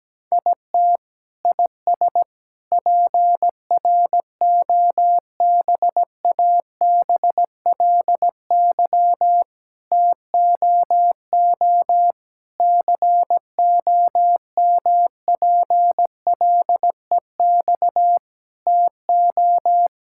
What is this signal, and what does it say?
Morse code, heard as a keyed tone, about 17 wpm, 700 Hz. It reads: IT IS PROBABLY TOO COMPLEX TO
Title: Sentences from Top 1000 Words - Review Part 6 - 17wpm